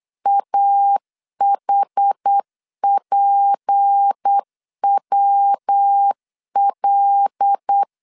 Morse code, heard as a keyed tone, about 8 wpm, 800 Hz. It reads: AHPWL
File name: LEZIONI / 6 PKX0 / 6 P 8wpm 3025